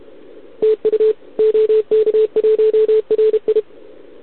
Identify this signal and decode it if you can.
Morse code, heard as a keyed tone, about 32 wpm, 440 Hz. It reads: TU OK1RI
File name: ARCHIV / OKDX2020 / OK1RIb